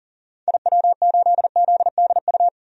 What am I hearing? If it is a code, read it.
Morse code, heard as a keyed tone, about 40 wpm, 700 Hz. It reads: IW8ZDU
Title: DXCC-100 Call Signs - 6 Characters - 40wpm